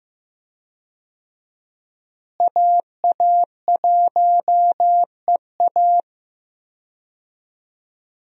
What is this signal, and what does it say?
Morse code, heard as a keyed tone, about 15 wpm, 700 Hz. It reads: AA1EA
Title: USA Call Signs - 5 Characters - Repeated 3x - 15wpm